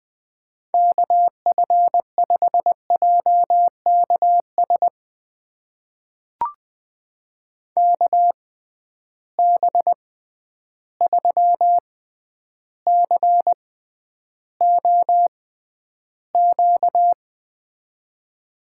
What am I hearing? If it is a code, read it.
Morse code, heard as a keyed tone, about 20 wpm, spaced as for 10 wpm, 700 Hz. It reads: KF5JKS KB3COQ
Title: USA Call Signs - 6 Characters - 6x Character Spacing - 20wpm